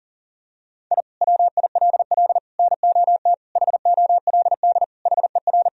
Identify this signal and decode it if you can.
Morse code, heard as a keyed tone, about 40 wpm, 700 Hz. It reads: I WILL NOT HOLD HER
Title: Sentences from Top 1000 Words - Review Part 2 - 40wpm